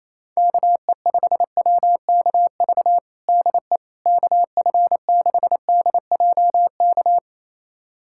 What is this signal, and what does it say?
Morse code, heard as a keyed tone, about 28 wpm, 700 Hz. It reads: KE5WKV DE KF6DJK
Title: USA Callsign DE Callsign - 6 Characters - 28wpm